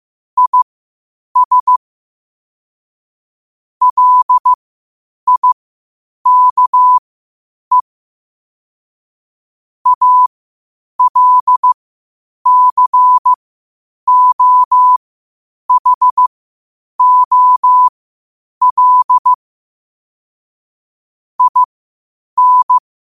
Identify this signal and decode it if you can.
Morse code, heard as a keyed tone, about 15 wpm, spaced as for 8 wpm, 1000 Hz. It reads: IS LIKE ALCOHOL IN
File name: qotdfiles / 8wpm